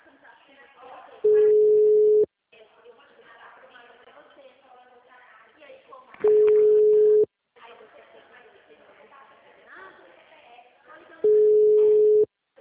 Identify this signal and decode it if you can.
Ringback tone (Europe, 425 Hz), with speech.
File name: out- -2003- - - .46370